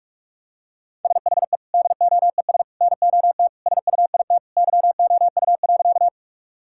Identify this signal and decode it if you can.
Morse code, heard as a keyed tone, about 46 wpm, 700 Hz. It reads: SHE DOES NOT SUIT YOU.
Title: Sentences from Top 1600 Words 45wpm